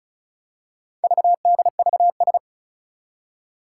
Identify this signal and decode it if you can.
Morse code, heard as a keyed tone, about 36 wpm, 700 Hz. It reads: VDVS
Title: Four Challenging Letter-Number Combinations 35wpm